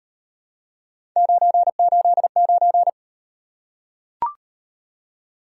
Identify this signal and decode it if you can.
Morse code, heard as a keyed tone, about 35 wpm, 700 Hz. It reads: 989